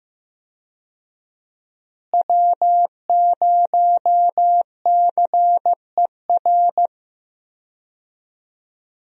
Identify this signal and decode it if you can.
Morse code, heard as a keyed tone, about 15 wpm, 700 Hz. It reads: W0CER
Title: USA Call Signs - 5 Characters - Repeated 3x - 15wpm